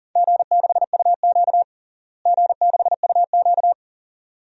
Morse code, heard as a keyed tone, about 37 wpm, 700 Hz. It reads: G6UQ G6UQ
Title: DXCC-100 Call Signs - 4 Characters - Repeated 2x - 40wpm